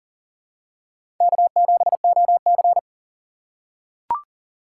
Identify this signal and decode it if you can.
Morse code, heard as a keyed tone, about 40 wpm, 700 Hz. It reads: K7OC
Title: USA Call Signs - 4 Characters - 2x Character Spacing - 40wpm